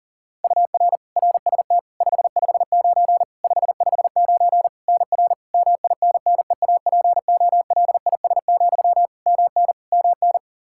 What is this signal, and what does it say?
Morse code, heard as a keyed tone, about 37 wpm, 700 Hz. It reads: UR RST 559 559 NR MINNEAPOLIS, MN MN
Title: Practice QSO - Exchange 1 - Beginning - 40wpm